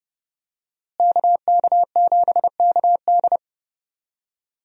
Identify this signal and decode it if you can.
Morse code, heard as a keyed tone, about 30 wpm, 700 Hz. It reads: KK7KD
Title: USA Call Signs - 5 Characters - Repeated 3x - 30wpm